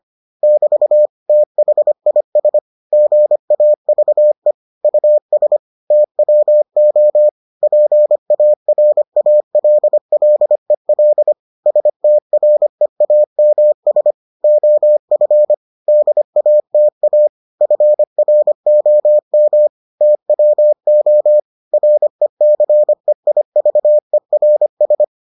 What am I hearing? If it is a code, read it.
Morse code, heard as a keyed tone, about 25 wpm, 600 Hz. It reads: <BT> THIS GAVE US TWO PARALLEL STREAMS OF DATA FROM TWO RECEIVERS